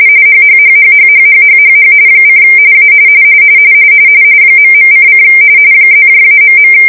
RTTY, about 45.45 baud. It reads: HHHHHHHHHH : A ..:: +-..